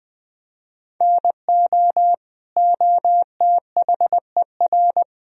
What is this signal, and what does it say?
Morse code, heard as a keyed tone, about 21 wpm, 700 Hz. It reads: NO OTHER
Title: Sentences from Top 1000 Words - Review Part 4 - 20wpm